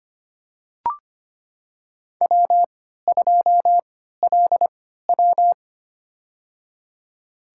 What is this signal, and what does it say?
Morse code, heard as a keyed tone, about 25 wpm, spaced as for 14 wpm, 700 Hz. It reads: W2LW
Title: USA Call Signs - 3x Character Spacing - 25wpm